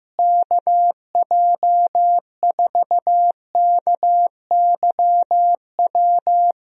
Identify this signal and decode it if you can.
Morse code, heard as a keyed tone, about 15 wpm, 700 Hz. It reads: KJ4KYW
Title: USA Call Signs - 6 Characters - Repeated 3x - 15wpm